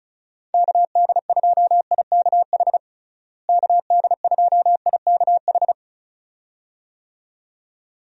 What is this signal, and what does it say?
Morse code, heard as a keyed tone, about 34 wpm, 700 Hz. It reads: KD2IKH KD2IKH
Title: USA Call Signs - 6 Characters - Repeated 2x - 35wpm